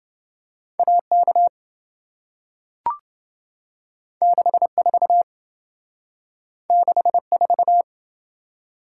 Morse code, heard as a keyed tone, about 27 wpm, 700 Hz. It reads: AK 64 64